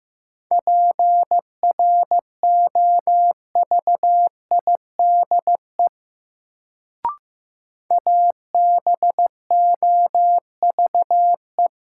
Morse code, heard as a keyed tone, about 15 wpm, 700 Hz. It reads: PROVIDE ABOVE
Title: Sentences from Top 1000 Words - Review Part 2 - New Word Drill - 15wpm